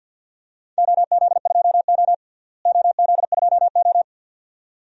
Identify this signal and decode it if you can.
Morse code, heard as a keyed tone, about 52 wpm, 700 Hz. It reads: OZ2O OZ2O
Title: DXCC-100 Call Signs - 4 Characters - Repeated 2x - 50wpm